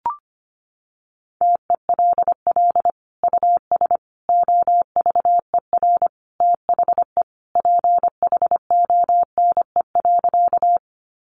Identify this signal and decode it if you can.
Morse code, heard as a keyed tone, about 25 wpm, 700 Hz. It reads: TELL US OVER THE PHONE.
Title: Sentences from Top 1100 Words 25wpm